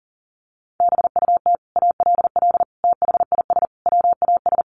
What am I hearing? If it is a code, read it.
Morse code, heard as a keyed tone, about 40 wpm, 700 Hz. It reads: BUT ALL THIS WAS
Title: Sentences from Top 1400 Words 40wpm